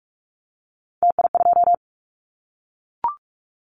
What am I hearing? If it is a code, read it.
Morse code, heard as a keyed tone, about 44 wpm, 700 Hz. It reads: TI2
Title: Callsign Prefixes for Top 100 DXCC - 45wpm